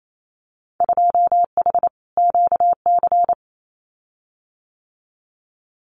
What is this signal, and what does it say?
Morse code, heard as a keyed tone, about 28 wpm, 700 Hz. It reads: 2H QC
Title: Winter Field Day Exchanges Encore 28wpm